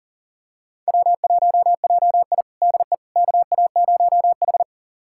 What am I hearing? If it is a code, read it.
Morse code, heard as a keyed tone, about 40 wpm, 700 Hz. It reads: W1JI DE KA0H
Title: USA Callsign DE Callsign - 4 Characters - 40wpm